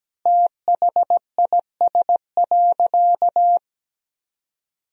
Morse code, heard as a keyed tone, about 17 wpm, 700 Hz. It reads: THIS.
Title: 4x Word-Spacing - Sentences from Top 200 Words - 17wpm